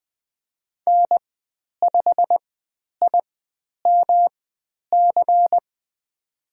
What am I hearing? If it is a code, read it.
Morse code, heard as a keyed tone, about 20 wpm, spaced as for 10 wpm, 700 Hz. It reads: N5IMC
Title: 109 Lesson 18g - Letter m - Call Signs 20wpm